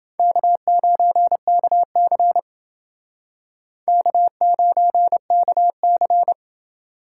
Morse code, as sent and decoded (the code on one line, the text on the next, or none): -.- ----. -.- -.-. / -.- ----. -.- -.-.
K9KC K9KC